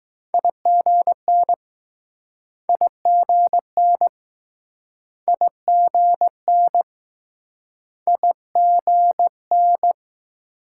Morse code, heard as a keyed tone, about 20 wpm, 700 Hz. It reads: IGN IGN IGN IGN